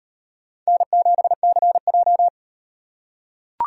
Morse code, heard as a keyed tone, about 38 wpm, 700 Hz. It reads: N7CJ